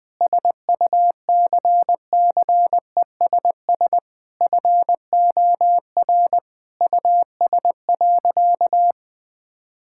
Morse code, heard as a keyed tone, about 20 wpm, 700 Hz. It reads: SUCCESS FOR US.
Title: Sentences from Top 1600 Words 20wpm